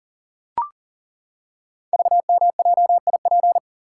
Morse code, heard as a keyed tone, about 39 wpm, 700 Hz. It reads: VMJIP
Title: Code Groups - 5 Letter:Numbers - 40wpm